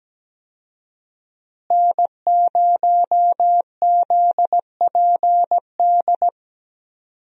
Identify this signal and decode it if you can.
Morse code, heard as a keyed tone, about 17 wpm, 700 Hz. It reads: N0ZPD
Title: USA Call Signs - 5 Characters - Repeated 3x - 17wpm